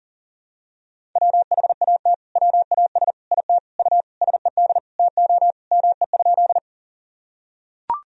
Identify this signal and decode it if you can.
Morse code, heard as a keyed tone, about 37 wpm, 700 Hz. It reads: WHAT WAS IT U SED TO ME?
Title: Sentences from Top 200 Words and Abbreviations 40wpm